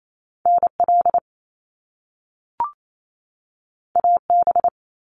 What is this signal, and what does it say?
Morse code, heard as a keyed tone, about 28 wpm, 700 Hz. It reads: NL AB
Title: Canadian Provinces and Territories 28wpm